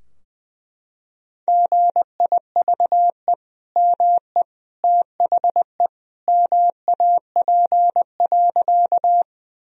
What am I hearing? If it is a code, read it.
Morse code, heard as a keyed tone, about 20 wpm, 700 Hz. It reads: GIVE ME THE MAP.
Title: Sentences from Top 1000 Words - Review Part 2 - 20wpm